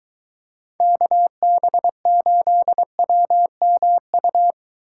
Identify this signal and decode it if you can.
Morse code, heard as a keyed tone, about 23 wpm, 700 Hz. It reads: KB8WMU